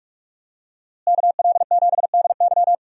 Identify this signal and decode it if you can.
Morse code, heard as a keyed tone, about 47 wpm, 700 Hz. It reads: KL7DY
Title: DXCC-100 Call Signs - 5 Characters - Repeated 2x - 45wpm